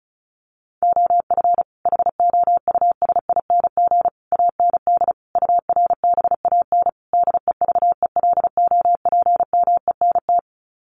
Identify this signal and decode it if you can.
Morse code, heard as a keyed tone, about 34 wpm, 700 Hz. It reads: OF HOUSING AND URBAN DEVELOPMENT
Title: Sets of 5 Words Encore2 35wpm